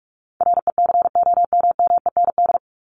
Morse code, heard as a keyed tone, about 44 wpm, 700 Hz. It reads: RECOMMEND
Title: Top 1001-1250 Words 45wpm